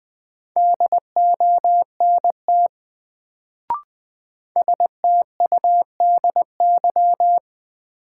Morse code, heard as a keyed tone, about 20 wpm, 700 Hz. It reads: DONT STUDY
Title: Sentences from Top 300 Words - New Word Drill - 20wpm